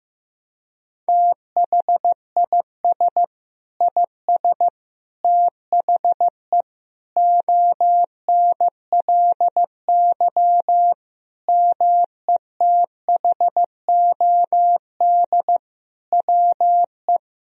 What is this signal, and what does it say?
Morse code, heard as a keyed tone, about 15 wpm, 700 Hz. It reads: THIS IS THE ONLY METHOD WE
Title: Sentences from Top 1000 Words - Review Part 3 - 15wpm